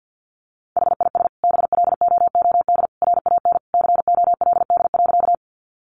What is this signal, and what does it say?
Morse code, heard as a keyed tone, about 50 wpm, 700 Hz. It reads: HIS BLOOD RAN COLD.
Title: Sentences from Top 1100 Words 50wpm